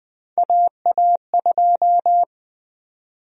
Morse code, heard as a keyed tone, about 20 wpm, 700 Hz. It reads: AA2
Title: USA Call Sign Prefixes - 20wpm